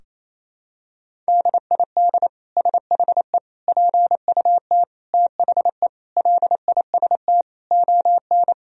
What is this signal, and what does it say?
Morse code, heard as a keyed tone, about 28 wpm, 700 Hz. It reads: DID SHE PUT THE LIST ON
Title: Sentences from Top 1000 Words - Review Part 2 - 28wpm